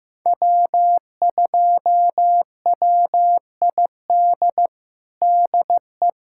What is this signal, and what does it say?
Morse code, heard as a keyed tone, about 15 wpm, 700 Hz. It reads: W2WID DE
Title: USA Callsign DE Callsign - 5 Characters - 15wpm